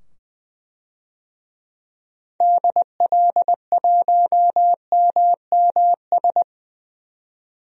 Morse code, heard as a keyed tone, about 20 wpm, 700 Hz. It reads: DL1MMS
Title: DXCC-100 Call Signs - 6 Characters - 20wpm